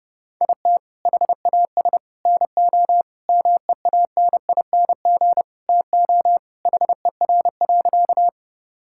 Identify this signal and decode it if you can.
Morse code, heard as a keyed tone, about 30 wpm, 700 Hz. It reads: IT HAS NO MEANING TO HER.
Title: Sentences from Top 1000 Words - Review Part 6 - 30wpm